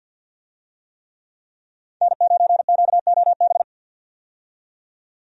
Morse code, heard as a keyed tone, about 50 wpm, 700 Hz. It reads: N9QOB
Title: USA Call Signs - 5 Characters - 50wpm